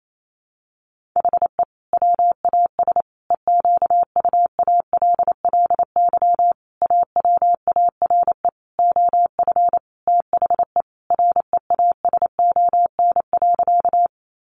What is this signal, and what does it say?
Morse code, heard as a keyed tone, about 28 wpm, 700 Hz. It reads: HE WAS EQUALLY AWARE OF THE REASON.
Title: Sentences from Top 1400 Words 28wpm